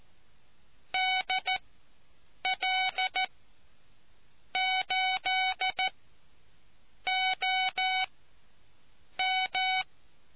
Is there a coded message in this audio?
Yes: Morse code from 0.9 to 9.8 s, heard as a keyed tone, about 14 wpm, spaced as for 6 wpm, 750 Hz. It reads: DL8OM